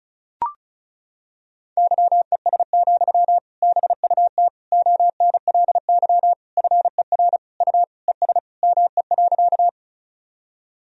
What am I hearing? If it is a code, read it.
Morse code, heard as a keyed tone, about 36 wpm, 700 Hz. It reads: YES, BUT ONLY FER U ES ME.
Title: Sentences from Top 500 Words and Abbreviations 35wpm